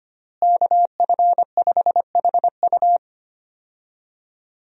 Morse code, heard as a keyed tone, about 25 wpm, 700 Hz. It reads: KF5HU
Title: USA Call Signs - 5 Characters - Repeated 2x - 25wpm